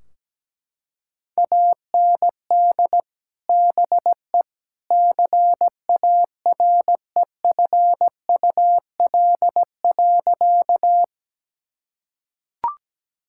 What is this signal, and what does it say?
Morse code, heard as a keyed tone, about 17 wpm, 700 Hz. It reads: AND BE CAREFUL.
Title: Sentences from Top 1600 Words 17wpm